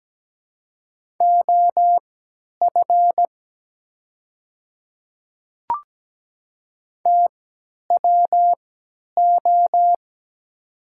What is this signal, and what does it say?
Morse code, heard as a keyed tone, about 17 wpm, spaced as for 10 wpm, 700 Hz. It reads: OF TWO
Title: 3x Character Spacing - Top 100 Words - 17wpm